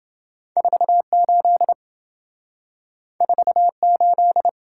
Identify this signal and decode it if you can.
Morse code, heard as a keyed tone, about 27 wpm, 700 Hz. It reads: 48 48